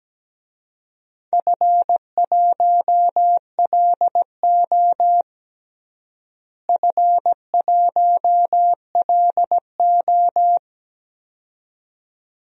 Morse code, heard as a keyed tone, about 17 wpm, 700 Hz. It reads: F1LO F1LO
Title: DXCC-100 Call Signs - 4 Characters - Repeated 2x - 17wpm